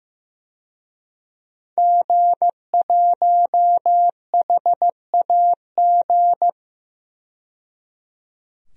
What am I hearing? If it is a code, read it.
Morse code, heard as a keyed tone, about 15 wpm, 700 Hz. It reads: G1HAG
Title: DXCC-100 Call Signs - 5 Characters - 15wpm